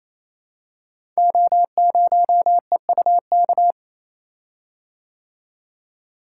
Morse code, heard as a keyed tone, about 28 wpm, 700 Hz. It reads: O0EUK
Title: Code Groups - 5 Letter:Numbers - 28wpm